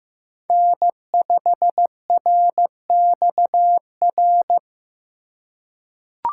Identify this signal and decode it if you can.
Morse code, heard as a keyed tone, about 15 wpm, 700 Hz. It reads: N5RXR
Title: USA Call Signs - 4x Character Spacing - 15wpm